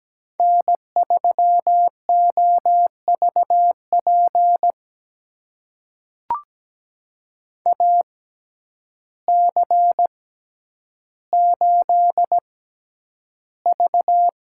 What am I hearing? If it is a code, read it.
Morse code, heard as a keyed tone, about 17 wpm, spaced as for 9 wpm, 700 Hz. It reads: N3OVP AC8V
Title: USA Call Signs - 5 Characters - 6x Character Spacing - 17wpm